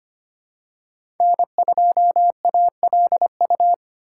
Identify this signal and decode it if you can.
Morse code, heard as a keyed tone, about 25 wpm, 700 Hz. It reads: N2ALU
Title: USA Call Signs - 3x Character Spacing - 25wpm